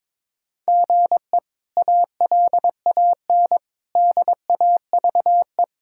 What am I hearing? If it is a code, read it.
Morse code, heard as a keyed tone, about 22 wpm, 700 Hz. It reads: GE ALAN DAVE
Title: K1USN SST QSO Practice - 22wpm